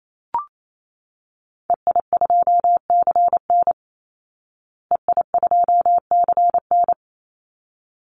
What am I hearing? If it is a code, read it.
Morse code, heard as a keyed tone, about 28 wpm, 700 Hz. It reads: EI2CN EI2CN
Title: Call Signs Repeated 3x - 28wpm